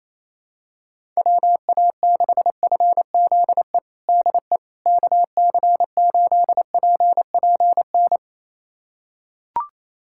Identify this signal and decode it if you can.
Morse code, heard as a keyed tone, about 28 wpm, 700 Hz. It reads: WA6FZE DE KC8PPN
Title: USA Callsign DE Callsign - 6 Characters - 28wpm